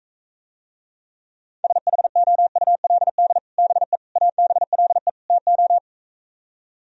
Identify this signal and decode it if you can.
Morse code, heard as a keyed tone, about 43 wpm, 700 Hz. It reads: SHOULD BE ABLE TO